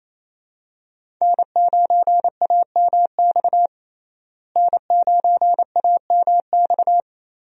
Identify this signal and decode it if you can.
Morse code, heard as a keyed tone, about 28 wpm, 700 Hz. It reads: N9AMX N9AMX
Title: USA Call Signs - 5 Characters - Repeated 3x - 28wpm